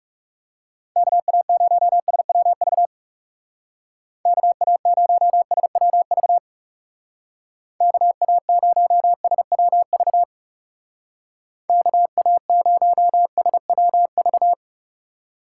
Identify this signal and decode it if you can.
Morse code, heard as a keyed tone, about 35 wpm, 700 Hz. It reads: KA0SWV KA0SWV KA0SWV KA0SWV
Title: Speed Racing - USA 2x3 Call Signs 30wpm